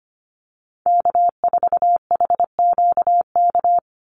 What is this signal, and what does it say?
Morse code, heard as a keyed tone, about 25 wpm, 700 Hz. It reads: K4HQK
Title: Call Signs - 25wpm